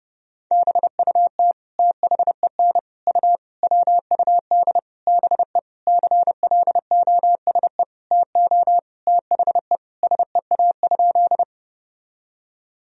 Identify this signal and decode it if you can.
Morse code, heard as a keyed tone, about 29 wpm, 700 Hz. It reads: BUT THEN U WUD BE CLOSE TO THE SEA?
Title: Sentences from Top 300 Words and Abbreviations 30wpm